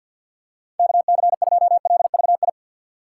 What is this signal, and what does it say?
Morse code, heard as a keyed tone, about 50 wpm, 700 Hz. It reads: KC2LVI
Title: USA Call Signs - 6 Characters - 6x Character Spacing - 50wpm